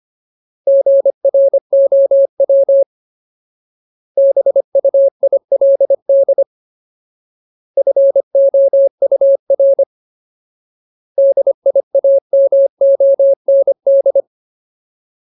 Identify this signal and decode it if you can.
Morse code, heard as a keyed tone, about 25 wpm, 550 Hz. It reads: GROW BUILD FOUR DIAMOND
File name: Ultimate Word List / 25WPM Ultimate